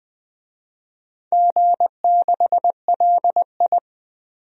Morse code, heard as a keyed tone, about 20 wpm, 700 Hz. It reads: G6LI
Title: DXCC-100 Call Signs - 4 Characters - 20wpm